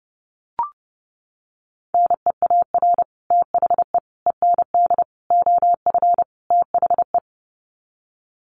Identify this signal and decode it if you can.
Morse code, heard as a keyed tone, about 30 wpm, 700 Hz. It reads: NEAR THE END OF THE
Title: Sets of 5 Words Encore 30wpm